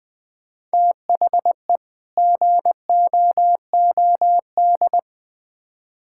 Morse code, heard as a keyed tone, about 21 wpm, 700 Hz. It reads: THE GOOD
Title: Build A Phrase - 5-Word Phrases - 20wpm